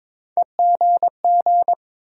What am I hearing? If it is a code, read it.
Morse code, heard as a keyed tone, about 22 wpm, 700 Hz. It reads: EGG
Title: Sentences from Top 1000 Words - Review Part 7 - New Word Drill - 22wpm